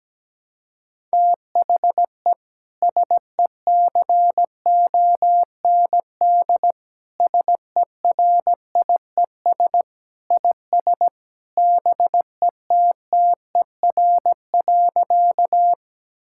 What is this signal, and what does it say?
Morse code, heard as a keyed tone, about 17 wpm, 700 Hz. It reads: THE SECOND SERIES IS BETTER.
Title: Sentences from Top 1000 Words - Review Part 5 - 17wpm